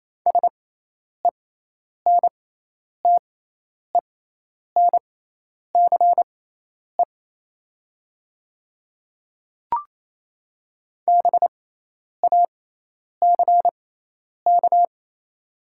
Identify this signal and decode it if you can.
Morse code, heard as a keyed tone, about 28 wpm, spaced as for 10 wpm, 690 Hz. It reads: SENTENCE BACK
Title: 6x Character Spacing - Top 200 Words - 28wpm